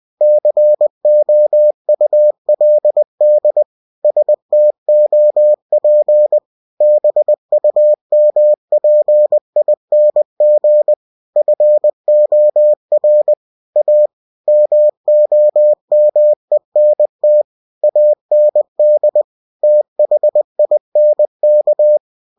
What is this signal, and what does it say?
Morse code, heard as a keyed tone, about 20 wpm, 600 Hz. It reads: COULD STOP BUMPING FOR A MOMENT AND THINK
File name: cw-books / winnie-the-pooh / winnie-20wpm-page-0000